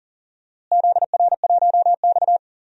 Morse code, heard as a keyed tone, about 40 wpm, 700 Hz. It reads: ZR1X
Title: DXCC-100 Call Signs - 4 Characters - Repeated 2x - 40wpm